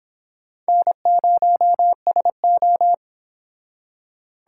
Morse code, heard as a keyed tone, about 26 wpm, 700 Hz. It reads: N0SO